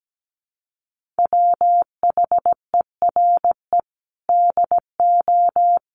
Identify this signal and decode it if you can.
Morse code, heard as a keyed tone, about 17 wpm, 700 Hz. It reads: WHERE DO
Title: Sentences from Top 200 Words 17wpm